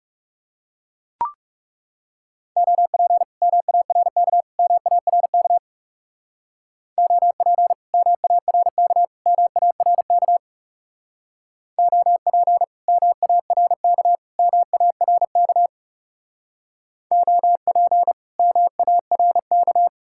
Morse code, heard as a keyed tone, about 39 wpm, 700 Hz. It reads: OP MARK MARK OP MARK MARK OP MARK MARK OP MARK